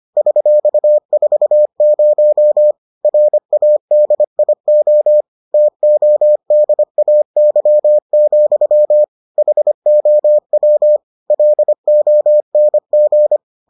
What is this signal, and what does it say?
Morse code, heard as a keyed tone, about 25 wpm, 600 Hz. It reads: $40 RADIO TODAY, HOW LONG